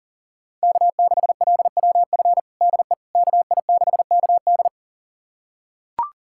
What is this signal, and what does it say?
Morse code, heard as a keyed tone, about 40 wpm, 700 Hz. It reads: K6LWF DE KI6KD
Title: USA Callsign DE Callsign - 5 Characters - 40wpm